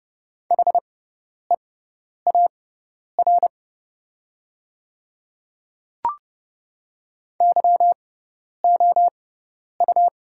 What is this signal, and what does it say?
Morse code, heard as a keyed tone, about 30 wpm, spaced as for 10 wpm, 700 Hz. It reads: HEAR YOU
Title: 6x Character Spacing - Top 500 Words - 30wpm